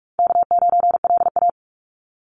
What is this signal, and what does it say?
Morse code, heard as a keyed tone, about 46 wpm, 700 Hz. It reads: K9LA
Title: Easy Call Signs - 45wpm